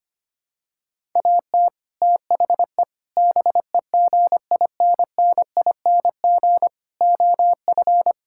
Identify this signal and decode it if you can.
Morse code, heard as a keyed tone, about 25 wpm, 700 Hz. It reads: AT THE BEGINNING OF
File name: practice / 25wpm / Sentences with 3-Word Phrases - 25wpm